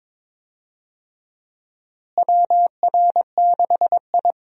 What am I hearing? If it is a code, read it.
Morse code, heard as a keyed tone, about 22 wpm, 700 Hz. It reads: WR6I
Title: USA Call Signs - 4 Characters - Repeated 2x - 22wpm